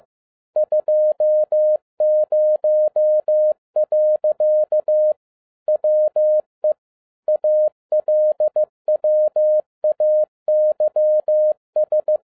Morse code, heard as a keyed tone, about 15 wpm, 610 Hz. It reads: 20. WE ALWAYS
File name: 2012-01-mp3 / n17-SUMMARY OF FEA NET IN 2011-15wpm